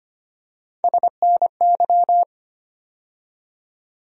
Morse code, heard as a keyed tone, about 25 wpm, 700 Hz. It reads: SNY
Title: USA Call Sign Postfixes - 3 Letters 25wpm